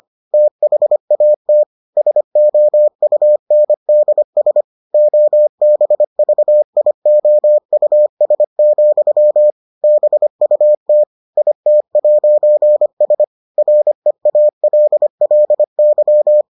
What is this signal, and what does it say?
Morse code, heard as a keyed tone, about 25 wpm, 600 Hz. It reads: THAT SOUNDS OBVIOUS, BUT IT'S REALLY